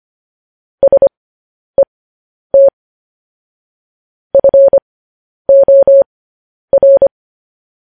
Morse code, heard as a keyed tone, about 25 wpm, spaced as for 10 wpm, 550 Hz. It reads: SET FOR
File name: podcast / News headlines-1x-25x10